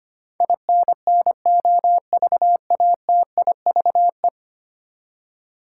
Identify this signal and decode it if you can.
Morse code, heard as a keyed tone, about 25 wpm, 700 Hz. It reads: INNOVATIVE
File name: Ten Letter Words - 25wpm